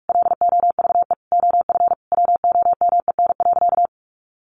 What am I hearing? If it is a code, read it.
Morse code, heard as a keyed tone, about 44 wpm, 700 Hz. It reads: LOVE OF WOMEN.
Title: Sentences from Top 1400 Words 45wpm